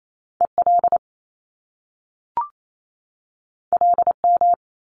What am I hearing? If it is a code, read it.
Morse code, heard as a keyed tone, about 28 wpm, 700 Hz. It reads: EL LM
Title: Top Two Letter Combinations in Words 28wpm